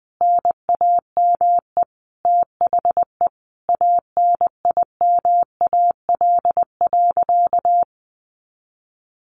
Morse code, heard as a keyed tone, about 20 wpm, 700 Hz. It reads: NAME THE ANIMAL.
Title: A Sound Of Thunder 20wpm